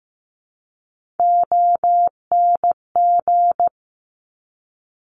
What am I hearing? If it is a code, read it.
Morse code, heard as a keyed tone, about 15 wpm, 700 Hz. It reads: ONG
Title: Top Three Letter Combinations in Words 15wpm